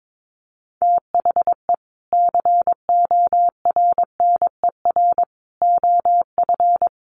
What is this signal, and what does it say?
Morse code, heard as a keyed tone, about 22 wpm, 700 Hz. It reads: THE CORNER OF
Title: Sets of 5 Words Encore2 22wpm